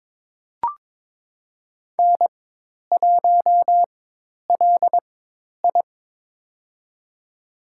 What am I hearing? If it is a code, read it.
Morse code, heard as a keyed tone, about 22 wpm, spaced as for 10 wpm, 700 Hz. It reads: N1LI
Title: USA Call Signs - 4 Characters - 4x Character Spacing - 22wpm